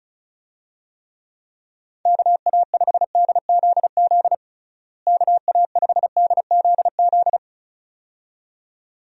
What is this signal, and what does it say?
Morse code, heard as a keyed tone, about 36 wpm, 700 Hz. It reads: KA5DZZ KA5DZZ
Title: USA Call Signs - 6 Characters - Repeated 2x - 35wpm